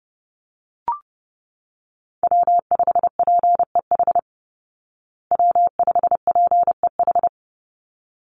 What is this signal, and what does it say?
Morse code, heard as a keyed tone, about 30 wpm, 700 Hz. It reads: W5PEH W5PEH
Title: Call Signs Repeated 3x - 30wpm